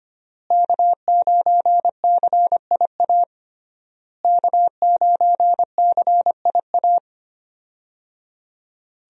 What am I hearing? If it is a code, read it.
Morse code, heard as a keyed tone, about 25 wpm, 700 Hz. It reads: K9CIA K9CIA
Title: USA Call Signs - 5 Characters - Repeated 3x - 25wpm